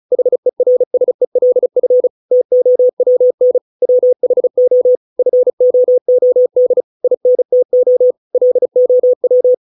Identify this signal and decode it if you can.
Morse code, heard as a keyed tone, about 34 wpm, 500 Hz. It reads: HERSELF TOWN WHO FOOD INTO LOW